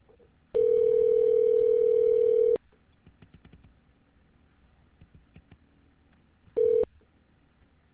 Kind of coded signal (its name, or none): telephone call progress tone